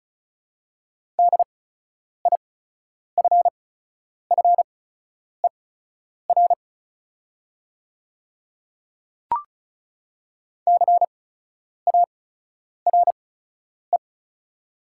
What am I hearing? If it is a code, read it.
Morse code, heard as a keyed tone, about 36 wpm, spaced as for 10 wpm, 700 Hz. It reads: DIFFER CARE
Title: 8x Character Spacing - Top 300 Words - 35wpm